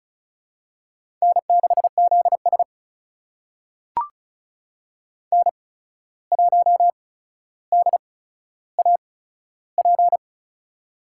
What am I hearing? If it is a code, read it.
Morse code, heard as a keyed tone, about 36 wpm, spaced as for 14 wpm, 700 Hz. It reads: N6ZS N1DAP
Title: USA Call Signs - 8x Character Spacing - 35wpm